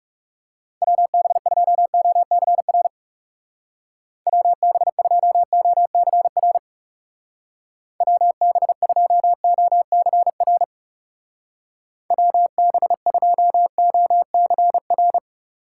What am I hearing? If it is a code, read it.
Morse code, heard as a keyed tone, about 36 wpm, 700 Hz. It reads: WB2OCR WB2OCR WB2OCR WB2OCR
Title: Speed Racing - USA 2x3 Call Signs 30wpm